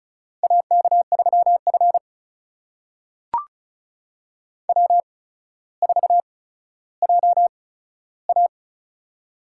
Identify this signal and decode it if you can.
Morse code, heard as a keyed tone, about 36 wpm, spaced as for 15 wpm, 700 Hz. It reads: AK3F W4JA
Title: USA Call Signs - 4 Characters - 8x Character Spacing - 35wpm